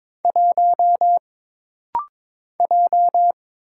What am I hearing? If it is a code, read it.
Morse code, heard as a keyed tone, about 22 wpm, 700 Hz. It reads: 1 J
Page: This rapid-fire timing is geared to push your ICR (Instant Character Recognition) skills.